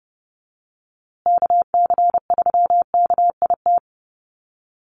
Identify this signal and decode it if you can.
Morse code, heard as a keyed tone, about 30 wpm, 700 Hz. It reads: KC3KIT
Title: USA Call Signs Repeated 3x 30wpm